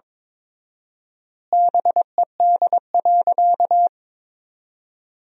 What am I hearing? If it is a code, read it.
Morse code, heard as a keyed tone, about 22 wpm, 700 Hz. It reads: BED.
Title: 4x Word-Spacing - Sentences from Top 500 Words - 22wpm